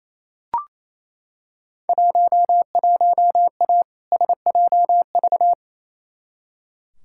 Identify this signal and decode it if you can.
Morse code, heard as a keyed tone, about 28 wpm, 700 Hz. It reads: 11A SJV
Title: Field Day Exchanges Encore - 28wpm